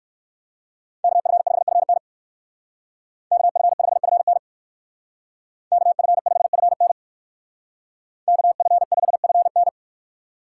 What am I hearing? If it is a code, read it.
Morse code, heard as a keyed tone, about 61 wpm, 700 Hz. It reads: KF5FN KF5FN KF5FN KF5FN